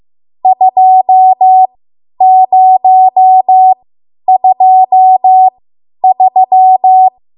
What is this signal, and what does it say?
Morse code, heard as a keyed tone, about 15 wpm, spaced as for 10 wpm, 750 Hz. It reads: 2023
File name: Morse / 10WPM